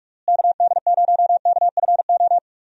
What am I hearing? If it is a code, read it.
Morse code, heard as a keyed tone, about 46 wpm, 700 Hz. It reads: KD0KFO
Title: USA Call Signs - 6 Characters - Repeated 3x - 45wpm